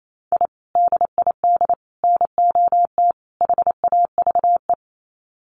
Morse code, heard as a keyed tone, about 28 wpm, 700 Hz. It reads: I DID NOT HAVE
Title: Sets of 4 Words 28wpm